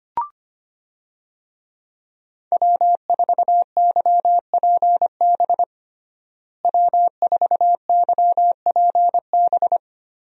Morse code, heard as a keyed tone, about 25 wpm, 700 Hz. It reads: W4YPB W4YPB
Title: USA Call Signs - 5 Characters - Repeated 2x - 25wpm